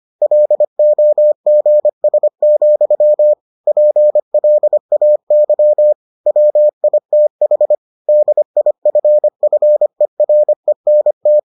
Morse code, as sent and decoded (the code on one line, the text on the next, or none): .-.. --- --. ... --..-- / .--. .-.. .- -.-- / .-- .. - .... / -.. .. ..-. ..-. .-. -. -
LOGS, PLAY WITH DIFFERENT